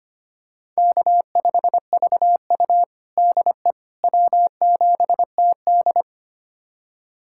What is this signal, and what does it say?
Morse code, heard as a keyed tone, about 25 wpm, 700 Hz. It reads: K5VU DE W7TD
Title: USA Callsign DE Callsign - 4 Characters - 25wpm